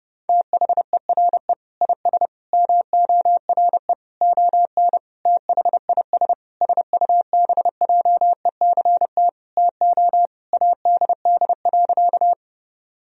Morse code, heard as a keyed tone, about 30 wpm, 700 Hz. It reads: THERE IS MORE ON THIS SUBJECT TO ADD.
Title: Sentences from Top 1000 Words - Review Part 1 - 30wpm